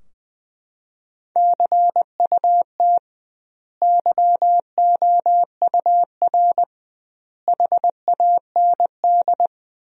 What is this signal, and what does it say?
Morse code, heard as a keyed tone, about 20 wpm, 700 Hz. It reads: CUT YOUR HAND
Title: 2x Word-Spacing - Sentences from Top 300 Words - 20wpm